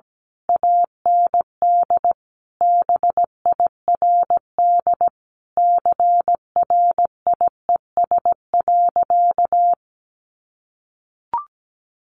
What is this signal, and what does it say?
Morse code, heard as a keyed tone, about 17 wpm, 700 Hz. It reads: AND BIRD CRIES.
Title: A Sound Of Thunder 17wpm